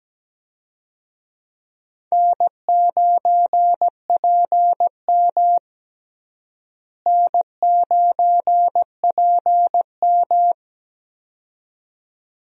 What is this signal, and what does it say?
Morse code, heard as a keyed tone, about 17 wpm, 700 Hz. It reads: N9PM N9PM
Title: USA Call Signs - 4 Characters - Repeated 2x - 17wpm